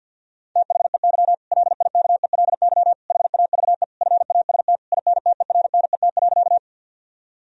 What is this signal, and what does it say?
Morse code, heard as a keyed tone, about 50 wpm, 700 Hz. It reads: THEY LIKELY HAVE FAST INTERNET.
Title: Sentences from Top 1000 Words - Review Part 4 - 50wpm